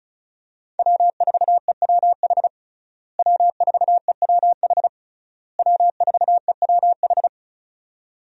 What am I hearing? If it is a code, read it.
Morse code, heard as a keyed tone, about 34 wpm, 700 Hz. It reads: W4EWH W4EWH W4EWH
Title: USA Call Signs - 5 Characters - Repeated 3x - 35wpm